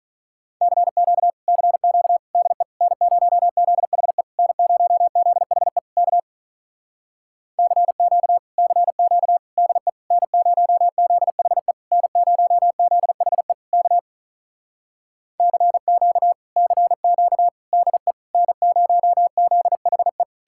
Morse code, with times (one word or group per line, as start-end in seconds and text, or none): 0.6-1.3 CQ
1.5-2.2 CQ
2.4-2.6 DE
2.8-4.2 N0ZHE
4.4-5.8 N0ZHE
6.0-6.2 K
7.6-8.4 CQ
8.6-9.4 CQ
9.6-9.9 DE
10.1-11.7 N0ZHE
11.9-13.5 N0ZHE
13.7-14.0 K
15.4-16.3 CQ
16.6-17.5 CQ
17.7-18.1 DE
18.3-20.2 N0ZHE